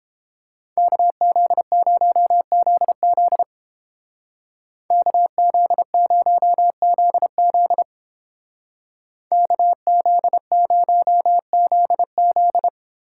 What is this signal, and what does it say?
Morse code, heard as a keyed tone, about 29 wpm, 700 Hz. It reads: KZ0ZZ KZ0ZZ KZ0ZZ